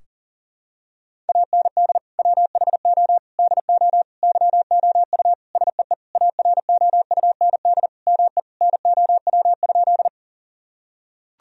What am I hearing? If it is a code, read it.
Morse code, heard as a keyed tone, about 40 wpm, 700 Hz. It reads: AND WHO DO YOU SEE AROUND ME NOW?
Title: Sentences from Top 1000 Words - Review Part 3 - 40wpm